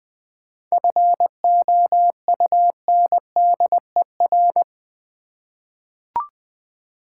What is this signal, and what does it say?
Morse code, heard as a keyed tone, about 20 wpm, 700 Hz. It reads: FOUNDER
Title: 7 Letter Words - First 3 Letters Given - 20wpm